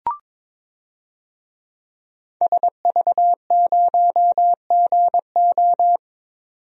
Morse code, heard as a keyed tone, about 22 wpm, 700 Hz. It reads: SV0GO
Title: DXCC-100 Call Signs - 5 Characters - 22wpm